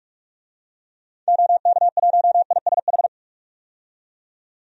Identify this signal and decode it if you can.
Morse code, heard as a keyed tone, about 41 wpm, 700 Hz. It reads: OK1ISH
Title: DXCC-100 Call Signs - 6 Characters - 45wpm